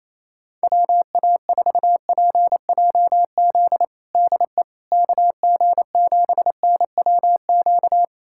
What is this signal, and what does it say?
Morse code, heard as a keyed tone, about 28 wpm, 700 Hz. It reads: WA4PJZ DE KG7NWQ
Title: USA Callsign DE Callsign 28wpm